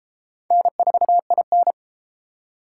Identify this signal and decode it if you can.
Morse code, heard as a keyed tone, about 31 wpm, 700 Hz. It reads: N4IN